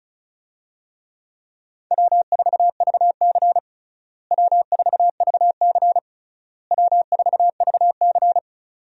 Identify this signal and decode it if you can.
Morse code, heard as a keyed tone, about 34 wpm, 700 Hz. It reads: W4VC W4VC W4VC
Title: USA Call Signs - 4 Characters - Repeated 3x - 35wpm